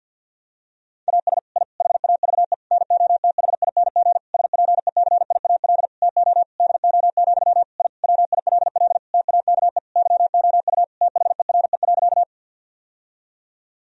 Morse code, heard as a keyed tone, about 50 wpm, 700 Hz. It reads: AS I HAVE NOTHING SPECIAL TO DO, I WILL TAKE YOU THERE.
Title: Sentences from Top 1000 Words - Review Part 3 - 50wpm